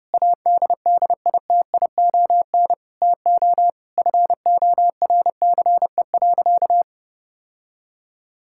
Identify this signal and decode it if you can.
Morse code, heard as a keyed tone, about 30 wpm, 700 Hz. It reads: ADDITION TO FORCE.
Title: Sentences from Top 1000 Words - Review Part 6 - 30wpm